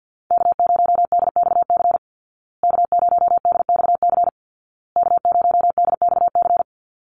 Morse code, heard as a keyed tone, about 50 wpm, 700 Hz. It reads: K0DXC K0DXC K0DXC
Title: Call Signs Repeated 2x - 50wpm